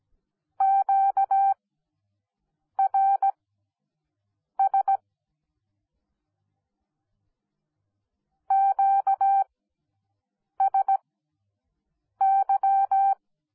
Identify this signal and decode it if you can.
Morse code, heard as a keyed tone, about 17 wpm, spaced as for 6 wpm, 780 Hz. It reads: QRS QSY
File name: Onlinelehrgang / morsen / DJ4UF Morsen10